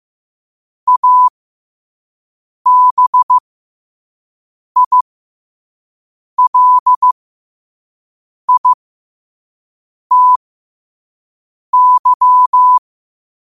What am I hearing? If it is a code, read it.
Morse code, heard as a keyed tone, about 15 wpm, spaced as for 5 wpm, 1000 Hz. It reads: ABILITY
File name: qotdfiles / 5wpm